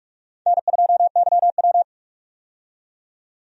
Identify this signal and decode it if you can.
Morse code, heard as a keyed tone, about 43 wpm, 700 Hz. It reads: N2YW
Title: USA Call Signs - 4 Characters - 45wpm